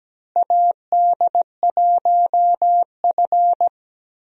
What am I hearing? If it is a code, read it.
Morse code, heard as a keyed tone, about 17 wpm, 700 Hz. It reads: AD1F
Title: USA Call Signs - 4 Characters - Repeated 3x - 17wpm